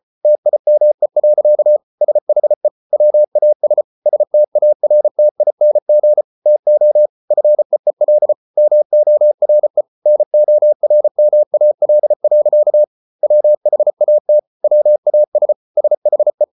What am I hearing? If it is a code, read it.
Morse code, heard as a keyed tone, about 34 wpm, 600 Hz. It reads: TIME. SHE WAS STARTING TO FEEL MORE NORMAL. WHAT WAS SHE